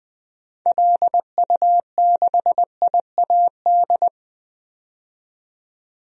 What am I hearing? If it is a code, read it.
Morse code, heard as a keyed tone, about 20 wpm, 700 Hz. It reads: LU6IAD
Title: DXCC-100 Call Signs - 6 Characters - 20wpm